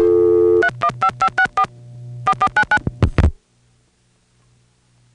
DTMF, heard as telephone touch tones, tones of about 70 ms. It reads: 315564 4499